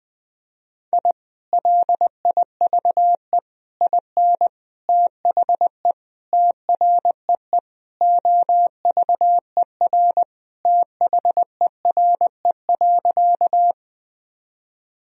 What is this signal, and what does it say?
Morse code, heard as a keyed tone, about 20 wpm, 700 Hz. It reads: I LIVE IN THE TREE OVER THERE.
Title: Sentences from Top 1000 Words - Review Part 1 - 20wpm